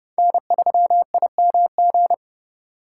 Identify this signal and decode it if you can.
Morse code, heard as a keyed tone, about 36 wpm, 700 Hz. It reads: N3IMG
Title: Speed Racing - USA 1x3 Call Signs 30wpm